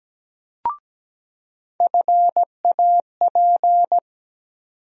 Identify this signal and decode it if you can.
Morse code, heard as a keyed tone, about 17 wpm, 700 Hz. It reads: FAP
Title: USA Call Sign Postfixes - 3 Letters 17wpm